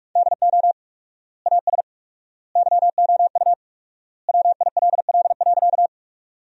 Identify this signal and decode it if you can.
Morse code, heard as a keyed tone, about 46 wpm, 700 Hz. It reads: DO AS YOU WILL.
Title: 4x Word-Spacing - Sentences from Top 100 Words - 45wpm